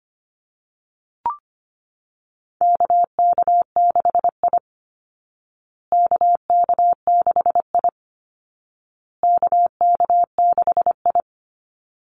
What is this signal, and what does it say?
Morse code, heard as a keyed tone, about 25 wpm, 700 Hz. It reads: KK6I KK6I KK6I
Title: Call Signs Repeated 2x - 25wpm